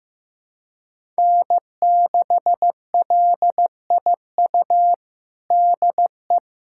Morse code, heard as a keyed tone, about 16 wpm, 700 Hz. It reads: N6LIU DE
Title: USA Callsign DE Callsign - 5 Characters - 15wpm